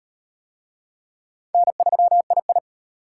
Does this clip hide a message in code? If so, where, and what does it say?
Yes: Morse code from 1.5 to 2.6 s, heard as a keyed tone, about 38 wpm, 700 Hz. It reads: N3II